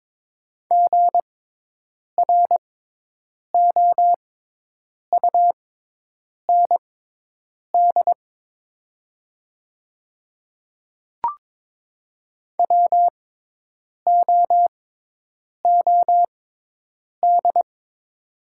Morse code, heard as a keyed tone, about 22 wpm, spaced as for 8 wpm, 700 Hz. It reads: GROUND WOOD
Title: 6x Character Spacing - Top 500 Words - 22wpm